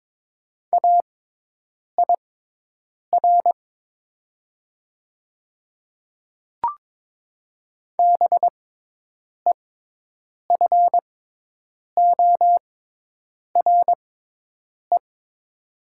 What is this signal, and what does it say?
Morse code, heard as a keyed tone, about 22 wpm, spaced as for 8 wpm, 700 Hz. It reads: AIR BEFORE
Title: 6x Character Spacing - Top 200 Words - 22wpm